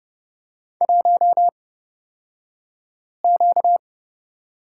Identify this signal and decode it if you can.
Morse code, heard as a keyed tone, about 30 wpm, spaced as for 5 wpm, 700 Hz. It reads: 1Q
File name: Lesson 38d - Letter q - Two Characters 30wpm